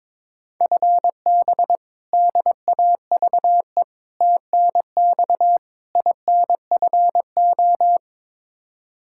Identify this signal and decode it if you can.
Morse code, heard as a keyed tone, about 22 wpm, 700 Hz. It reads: FB DAVE TNX INFO
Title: K1USN SST QSO Practice 22wpm